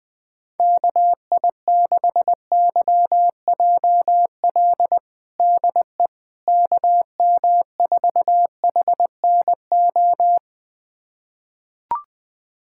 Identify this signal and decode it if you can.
Morse code, heard as a keyed tone, about 20 wpm, 700 Hz. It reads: KI6YJL DE KM4HNO
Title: USA Callsign DE Callsign - 6 Characters - 20wpm